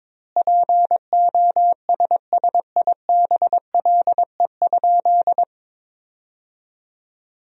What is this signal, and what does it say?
Morse code, heard as a keyed tone, about 22 wpm, 700 Hz. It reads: POSSIBLE?
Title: 2x Word-Spacing - Sentences from Top 500 Words - 22wpm